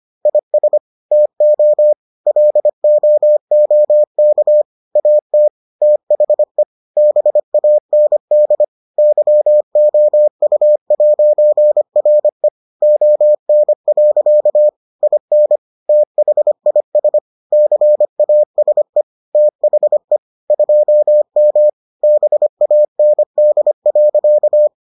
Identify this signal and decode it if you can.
Morse code, heard as a keyed tone, about 25 wpm, 600 Hz. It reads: IS TO LOOK AT THE BAND YOU'RE ON. IN THIS CASE THE 2M BAND.